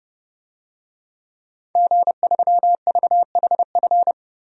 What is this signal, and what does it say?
Morse code, heard as a keyed tone, about 32 wpm, 700 Hz. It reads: G3VHF
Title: DXCC-100 Call Signs - 5 Characters - 30wpm